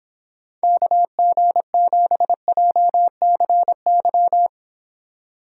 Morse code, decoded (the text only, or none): KG7JCY